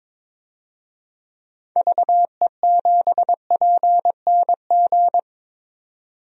Morse code, heard as a keyed tone, about 22 wpm, 700 Hz. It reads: VE7PNG
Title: DXCC-100 Call Signs - 6 Characters - 22wpm